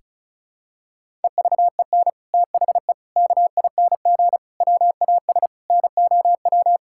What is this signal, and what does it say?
Morse code, heard as a keyed tone, about 36 wpm, 700 Hz. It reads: EVEN THE KING WAS NOW
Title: Sentences from Top 1000 Words - Review Part 6 - 35wpm